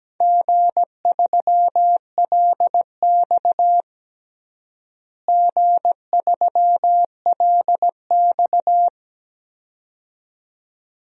Morse code, heard as a keyed tone, about 17 wpm, 700 Hz. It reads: G3LX G3LX
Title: DXCC-100 Call Signs - 4 Characters - Repeated 2x - 17wpm